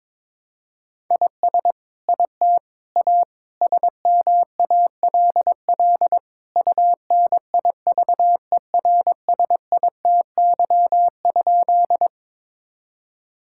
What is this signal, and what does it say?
Morse code, heard as a keyed tone, about 22 wpm, 700 Hz. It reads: IS IT A SMALL UNIVERSITY?
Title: Sentences from Top 1000 Words - Review Part 5 - 22wpm